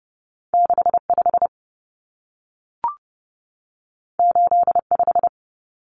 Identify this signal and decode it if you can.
Morse code, heard as a keyed tone, about 30 wpm, 700 Hz. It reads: 65 85
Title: Numbers - 2-Digit 30wpm